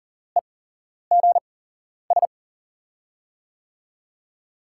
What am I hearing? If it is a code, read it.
Morse code, heard as a keyed tone, about 42 wpm, spaced as for 11 wpm, 700 Hz. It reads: EGS
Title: USA Call Signs - 8x Character Spacing - 40wpm